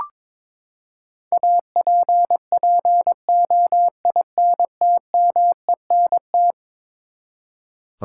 Morse code, heard as a keyed tone, about 22 wpm, 700 Hz. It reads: APPOINTMENT
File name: practice / 22wpm / Eleven Letter Words - 22wpm